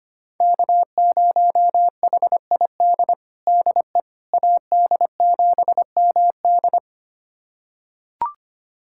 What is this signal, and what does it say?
Morse code, heard as a keyed tone, about 25 wpm, 700 Hz. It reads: K0HID DE AD7MD
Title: USA Callsign DE Callsign - 5 Characters - 25wpm